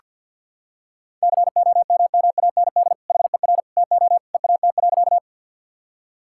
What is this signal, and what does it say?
Morse code, heard as a keyed tone, about 47 wpm, 700 Hz. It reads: COMMAND HER TO EAT.
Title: Sentences from Top 1600 Words 50wpm